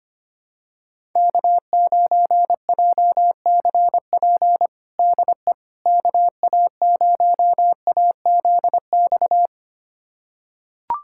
Morse code, heard as a keyed tone, about 25 wpm, 700 Hz. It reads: K9JCP DE KA0AZX
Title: USA Callsign DE Callsign 25wpm